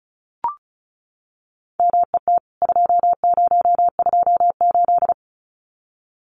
Morse code, heard as a keyed tone, about 34 wpm, 700 Hz. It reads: MET 2028
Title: CWT Exchange 35wpm